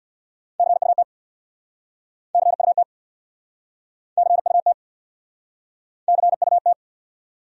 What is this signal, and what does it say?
Morse code, heard as a keyed tone, about 65 wpm, 700 Hz. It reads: CUT CUT CUT CUT